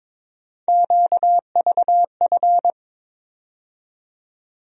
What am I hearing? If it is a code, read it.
Morse code, heard as a keyed tone, about 22 wpm, 700 Hz. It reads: QVF
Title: USA Call Sign Postfixes - 3 Letters 22wpm